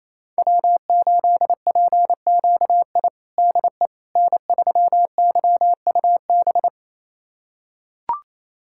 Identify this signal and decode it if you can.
Morse code, heard as a keyed tone, about 28 wpm, 700 Hz. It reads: W8PQI DE N3YUB
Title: USA Callsign DE Callsign - 5 Characters - 28wpm